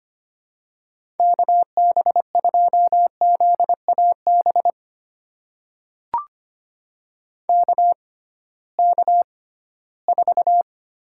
Morse code, heard as a keyed tone, about 25 wpm, spaced as for 16 wpm, 700 Hz. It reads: KB2ZAB KK4
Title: USA Call Signs - 6 Characters - 6x Character Spacing - 25wpm